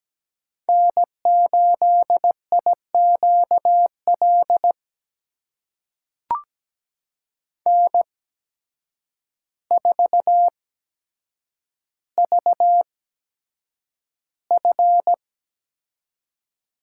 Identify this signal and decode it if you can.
Morse code, heard as a keyed tone, about 17 wpm, spaced as for 8 wpm, 700 Hz. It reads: N8IQL N4VF
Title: USA Call Signs - 5 Characters - 8x Character Spacing - 17wpm